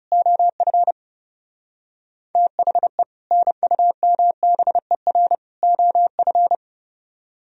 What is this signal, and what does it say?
Morse code, heard as a keyed tone, about 37 wpm, 700 Hz. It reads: OF THE NUMBER OF
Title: Speed-Racing - Sets of 3 Words 30wpm